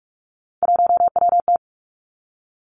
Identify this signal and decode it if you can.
Morse code, heard as a keyed tone, about 46 wpm, 700 Hz. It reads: 1WT
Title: Three Challenging Letter-Number Combinations 45wpm